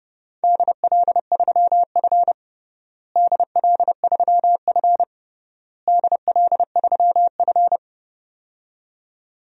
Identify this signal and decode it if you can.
Morse code, heard as a keyed tone, about 29 wpm, 700 Hz. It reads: DL3F DL3F DL3F
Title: DXCC-100 Call Signs - 4 Characters - Repeated 3x - 30wpm